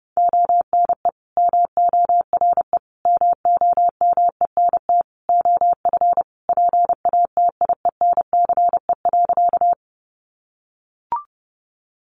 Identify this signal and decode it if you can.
Morse code, heard as a keyed tone, about 30 wpm, 700 Hz. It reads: ONE MORE MOMENT OF PATIENCE.
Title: Sentences from Top 1300 Words 30wpm